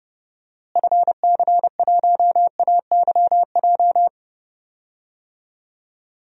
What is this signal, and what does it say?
Morse code, heard as a keyed tone, about 30 wpm, 700 Hz. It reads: FC1AYJ
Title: DXCC-100 Call Signs - 6 Characters - Repeated 3x - 30wpm